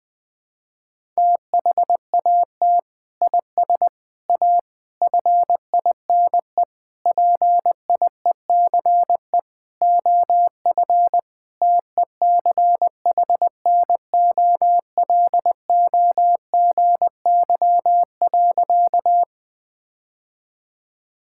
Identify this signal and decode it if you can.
Morse code, heard as a keyed tone, about 20 wpm, 700 Hz. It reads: THAT IS A FINE PIECE OF TECHNOLOGY.
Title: Sentences from Top 1000 Words - Review Part 5 - 20wpm